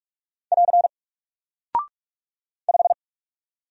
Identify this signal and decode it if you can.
Morse code, heard as a keyed tone, about 44 wpm, 700 Hz. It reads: <AR> 5
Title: Single Character (Letter:Number:Punctuation:Prosign) Alt 45wpm